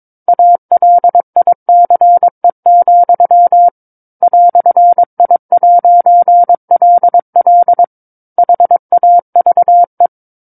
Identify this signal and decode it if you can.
Morse code, heard as a keyed tone, about 22 wpm, 700 Hz. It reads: ALICE, "I'LL HAVE
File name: 06-Chapter-06 22 WPM